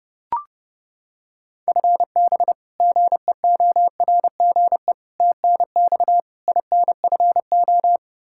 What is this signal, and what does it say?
Morse code, heard as a keyed tone, about 30 wpm, 700 Hz. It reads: FB GEORGE TNX INFO
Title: K1USN SST QSO Practice 30wpm